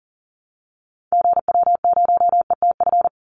Morse code, heard as a keyed tone, about 37 wpm, 700 Hz. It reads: GW0ETF
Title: Call Signs Repeated 3x - 40wpm